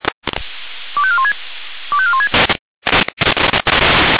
Five-tone selective call (ZVEI), heard as quick runs of five tones, tones of about 70 ms. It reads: 25416 25416
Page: /SDR/ZVEI/